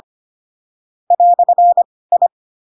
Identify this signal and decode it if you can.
Morse code, heard as a keyed tone, about 25 wpm, spaced as for 18 wpm, 700 Hz. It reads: "I
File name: Pinocchio / Pinocchio ch4 12 25 ik1hge